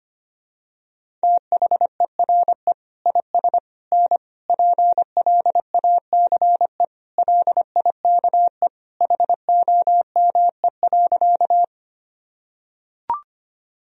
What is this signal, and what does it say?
Morse code, heard as a keyed tone, about 25 wpm, 700 Hz. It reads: THERE IS N PLACE LIKE HOME.
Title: Sentences from Top 200 Words and Abbreviations 25wpm